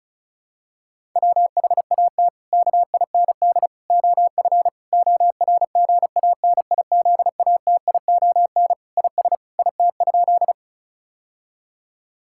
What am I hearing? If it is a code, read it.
Morse code, heard as a keyed tone, about 34 wpm, 700 Hz. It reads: WHAT KIND OF ORGANIZATION IS IT?
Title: Sentences from Top 1000 Words - Review Part 5 - 35wpm